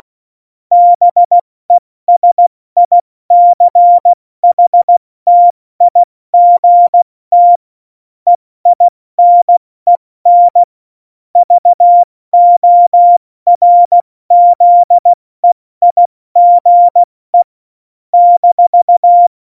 Morse code, heard as a keyed tone, about 16 wpm, spaced as for 14 wpm, 700 Hz. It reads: BESICHTIGT EINEN VORZEIGE -
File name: Morsekurs / Witze / Witz 14 Juli 2020, 70 F80